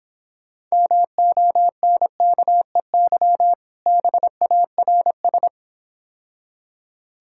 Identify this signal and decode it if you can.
Morse code, heard as a keyed tone, about 26 wpm, 700 Hz. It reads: MONKEY BARS